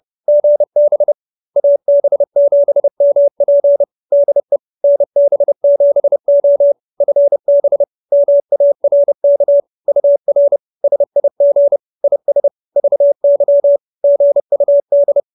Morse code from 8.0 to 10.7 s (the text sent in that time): MARK UR